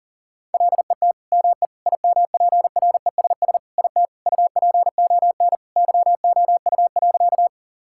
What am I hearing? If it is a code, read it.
Morse code, heard as a keyed tone, about 40 wpm, 700 Hz. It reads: LET ME IMPRESS IT UPON YOU.
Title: Sentences from Top 1600 Words 40wpm